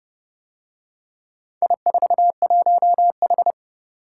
Morse code, heard as a keyed tone, about 28 wpm, 700 Hz. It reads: I41H